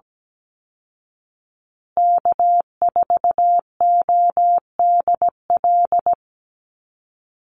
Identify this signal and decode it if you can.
Morse code, heard as a keyed tone, about 17 wpm, 700 Hz. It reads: K4ODL
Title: Call Signs Repeated 3x - 17wpm